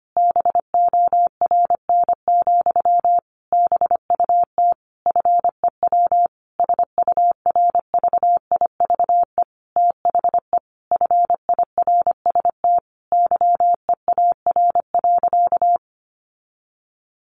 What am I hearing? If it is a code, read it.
Morse code, heard as a keyed tone, about 25 wpm, 700 Hz. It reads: BORN, BUT FEW SURVIVE THE FIRST YEAR.
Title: Sentences from Top 1300 Words 25wpm